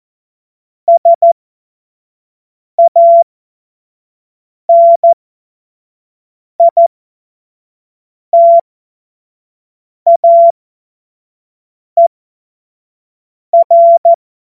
Morse code, heard as a keyed tone, about 14 wpm, spaced as for 5 wpm, 680 Hz. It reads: SANITAER